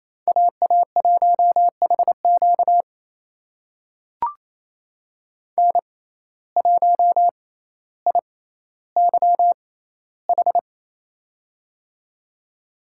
Morse code, heard as a keyed tone, about 28 wpm, spaced as for 14 wpm, 700 Hz. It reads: AA1HQ N1IYH
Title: USA Call Signs - 5 Characters - 6x Character Spacing - 28wpm